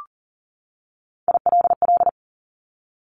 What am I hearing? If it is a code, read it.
Morse code, heard as a keyed tone, about 42 wpm, 690 Hz. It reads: ILL
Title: Top Three Letter Combinations in Words 40wpm